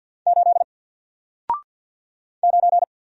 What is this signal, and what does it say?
Morse code, heard as a keyed tone, about 48 wpm, 700 Hz. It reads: 8 9
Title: Single Letter-Number Alt 50wpm